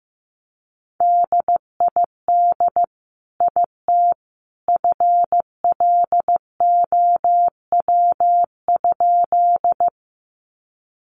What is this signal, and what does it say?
Morse code, heard as a keyed tone, about 15 wpm, 700 Hz. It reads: DID IT FLOW?
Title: Sentences from Top 1200 Words 15wpm